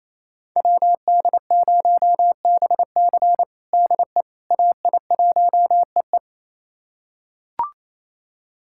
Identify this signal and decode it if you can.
Morse code, heard as a keyed tone, about 28 wpm, 700 Hz. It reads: WD0BC DE AI1EE
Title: USA Callsign DE Callsign - 5 Characters - 28wpm